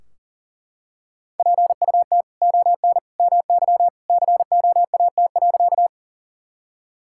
Morse code, heard as a keyed tone, about 37 wpm, 700 Hz. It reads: PUT ON MY COAT.
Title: Sentences from Top 1600 Words 40wpm